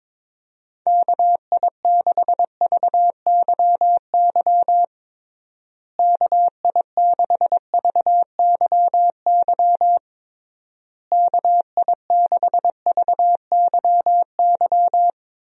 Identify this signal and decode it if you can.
Morse code, heard as a keyed tone, about 21 wpm, 700 Hz. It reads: KI6VYY KI6VYY KI6VYY
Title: USA Call Signs - 6 Characters - Repeated 3x - 22wpm